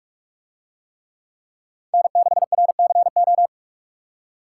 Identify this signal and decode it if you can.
Morse code, heard as a keyed tone, about 44 wpm, 700 Hz. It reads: N6RCO